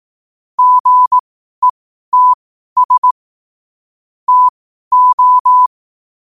Morse code, heard as a keyed tone, about 18 wpm, spaced as for 12 wpm, 1000 Hz. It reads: GETS TO
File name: qotdfiles / 12wpm